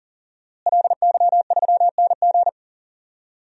Morse code, heard as a keyed tone, about 38 wpm, 700 Hz. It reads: LY3NG
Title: DXCC-100 Call Signs - 5 Characters - Repeated 2x - 40wpm